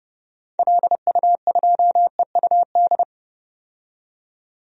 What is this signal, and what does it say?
Morse code, heard as a keyed tone, about 30 wpm, 700 Hz. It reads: LU2EUD
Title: DXCC-100 Call Signs - 6 Characters - 30wpm